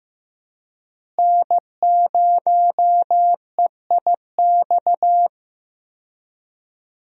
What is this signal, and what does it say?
Morse code, heard as a keyed tone, about 15 wpm, 700 Hz. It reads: N0EIX
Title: USA Call Signs - 5 Characters - Repeated 3x - 15wpm